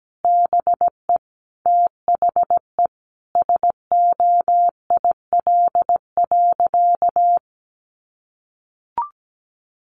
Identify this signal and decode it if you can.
Morse code, heard as a keyed tone, about 17 wpm, 700 Hz. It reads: BE THE SOIL.
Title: Sentences from Top 1500 Words 17wpm